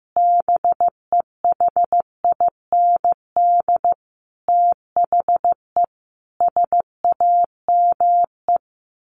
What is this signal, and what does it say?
Morse code, heard as a keyed tone, about 15 wpm, 700 Hz. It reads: BEHIND THE SAME
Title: A Sound Of Thunder 15wpm